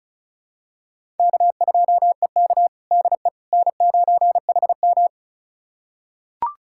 Morse code, heard as a keyed tone, about 36 wpm, 700 Hz. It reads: K2EK DE N9HM
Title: USA Callsign DE Callsign - 4 Characters - 35wpm